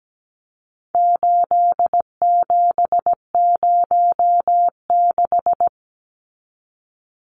Morse code, heard as a keyed tone, about 17 wpm, 700 Hz. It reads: 8706
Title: Numbers - 4-Digit 17wpm